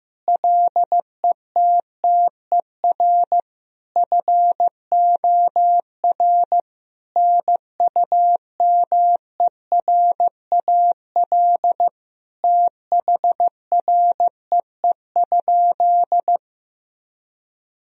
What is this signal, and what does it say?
Morse code, heard as a keyed tone, about 15 wpm, 700 Hz. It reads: LETTER FOR NUMERAL THREE?
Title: Sentences from Top 1000 Words - Review Part 2 - 15wpm